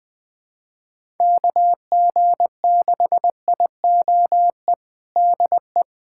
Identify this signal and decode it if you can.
Morse code, heard as a keyed tone, about 20 wpm, 700 Hz. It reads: KG6IOE DE
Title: USA Callsign DE Callsign 20wpm